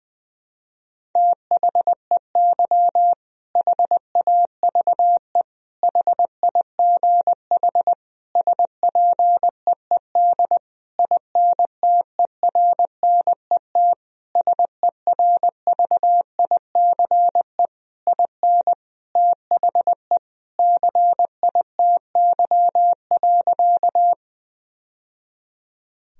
Morse code, heard as a keyed tone, about 20 wpm, 700 Hz. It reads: THEY HAVE HIGH SPEED INTERNET SERVICE IN THE CITY.
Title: Sentences from Top 1000 Words - Review Part 4 - 20wpm